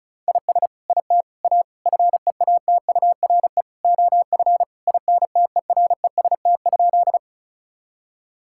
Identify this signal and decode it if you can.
Morse code, heard as a keyed tone, about 36 wpm, 700 Hz. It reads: IS IT A FEATURE OF INTEREST?
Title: Sentences from Top 1000 Words - Review Part 5 - 35wpm